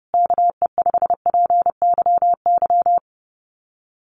Morse code, heard as a keyed tone, about 30 wpm, 700 Hz. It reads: KE5PYY
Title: USA Call Signs Repeated 2x 30wpm